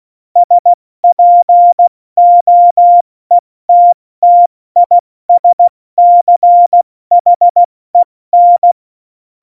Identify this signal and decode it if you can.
Morse code, heard as a keyed tone, about 16 wpm, spaced as for 14 wpm, 700 Hz. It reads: SPOETTISCHEN
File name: Morsekurs / Witze / Witz 11 Juli 17, Tempo 70 F80